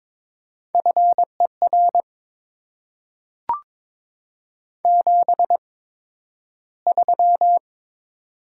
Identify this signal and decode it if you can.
Morse code, heard as a keyed tone, about 22 wpm, spaced as for 12 wpm, 700 Hz. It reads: FER 73
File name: Top 20 QSO Elements - 8x Character Spacing - 22wpm